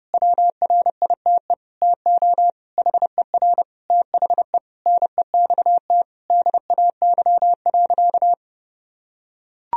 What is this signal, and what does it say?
Morse code, heard as a keyed tone, about 32 wpm, 700 Hz. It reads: WRITE TO HER THE NEXT DAY.
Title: Sentences from Top 1000 Words - Review Part 1 - 30wpm